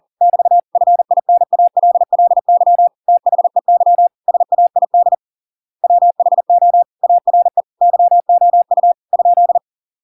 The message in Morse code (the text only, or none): <BT> FINALLY THEY SAID WHO ARE YOU ?